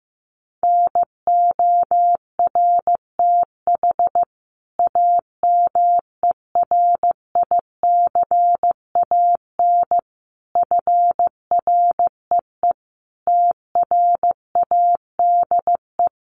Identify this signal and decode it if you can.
Morse code, heard as a keyed tone, about 15 wpm, 700 Hz. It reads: NORTH AMERICAN FREE TRADE
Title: Sets of 5 Words Encore 15wpm